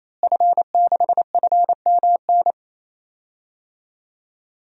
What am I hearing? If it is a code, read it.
Morse code, heard as a keyed tone, about 28 wpm, 700 Hz. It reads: F6FMN
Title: DXCC-100 Call Signs - 5 Characters - Repeated 3x - 28wpm